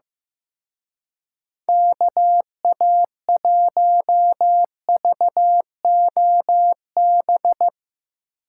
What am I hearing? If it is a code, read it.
Morse code, heard as a keyed tone, about 16 wpm, 700 Hz. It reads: KA1VOB
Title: Speed Racing - USA 2x3 Call Signs 15wpm